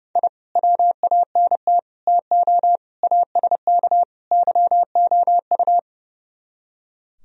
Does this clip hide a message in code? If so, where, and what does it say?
Yes: Morse code from 0.2 to 5.8 s, heard as a keyed tone, about 38 wpm, 700 Hz. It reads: I WANT TO ASK YOU